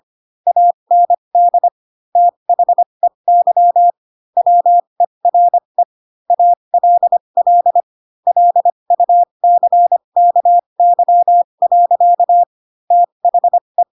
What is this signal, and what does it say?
Morse code, heard as a keyed tone, about 25 wpm, spaced as for 22 wpm, 700 Hz. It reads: AND THEY WERE ALL LUCKY. THE
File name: Pinocchio / Pinocchio ch3 18 25 ik1hge